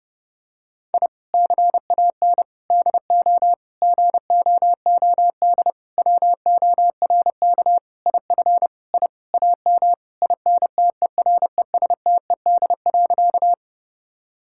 Morse code, heard as a keyed tone, about 30 wpm, 700 Hz. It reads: I CAN DO GOOD WORK IF I AM INTERESTED.
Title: Sentences from Top 1000 Words - Review Part 6 - 30wpm